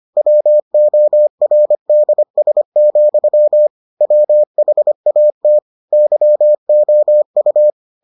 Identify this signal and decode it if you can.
Morse code, heard as a keyed tone, about 25 wpm, 600 Hz. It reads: WORDS, WHAT YOU